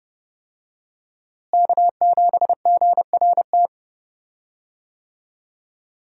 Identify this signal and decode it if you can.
Morse code, heard as a keyed tone, about 31 wpm, 700 Hz. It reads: K7GRT
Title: USA Call Signs - 5 Characters - Repeated 3x - 30wpm